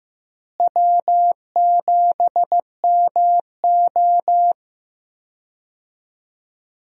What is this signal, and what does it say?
Morse code, heard as a keyed tone, about 15 wpm, 700 Hz. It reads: W7MO
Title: USA Call Signs - 4 Characters - Repeated 3x - 15wpm